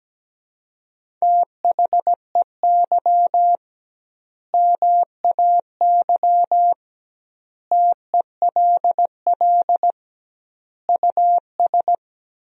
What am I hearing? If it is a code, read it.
Morse code, heard as a keyed tone, about 17 wpm, 700 Hz. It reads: THEY MAY TELL US
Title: 2x Word-Spacing - Sentences from Top 300 Words - 17wpm